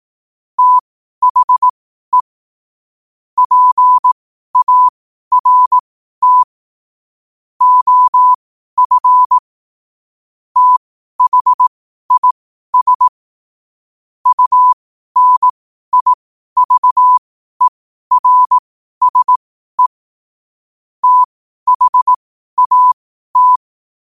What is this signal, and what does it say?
Morse code, heard as a keyed tone, about 18 wpm, spaced as for 12 wpm, 1000 Hz. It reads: THE PART OF THIS UNIVERSE THAT
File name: qotdfiles / 12wpm